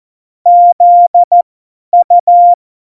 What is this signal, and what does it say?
Morse code, heard as a keyed tone, about 14 wpm, spaced as for 10 wpm, 700 Hz. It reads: ZU
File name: Morsekurs / Witze / Witz 05 Mai 2020, 50 F70